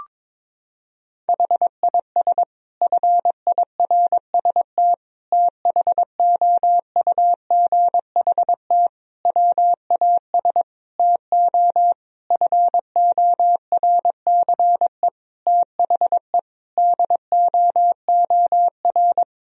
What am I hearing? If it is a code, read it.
Morse code, heard as a keyed tone, about 22 wpm, 700 Hz. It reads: HIS FIRST THOUGHT WAS TO FORCE THE DOOR
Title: Sentences from Top 1000 Words - Review Part 3 - 22wpm